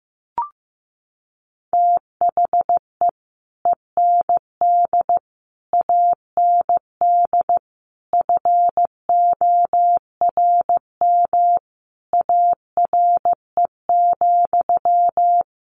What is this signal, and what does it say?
Morse code, heard as a keyed tone, about 15 wpm, 700 Hz. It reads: THE END AND FORM ARE,
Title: Sentences from Top 1400 Words 15wpm